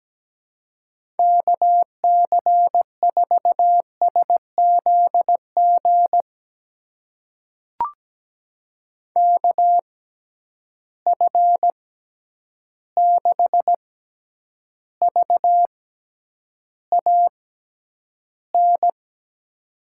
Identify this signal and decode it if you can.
Morse code, heard as a keyed tone, about 17 wpm, spaced as for 9 wpm, 700 Hz. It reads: KC4SZG KF6VAN
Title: USA Call Signs - 6 Characters - 6x Character Spacing - 17wpm